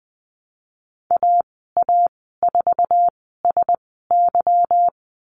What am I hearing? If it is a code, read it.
Morse code, heard as a keyed tone, about 20 wpm, spaced as for 15 wpm, 700 Hz. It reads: AA4SY
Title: 173 Lesson 28g - Number 9 - Call Signs 20wpm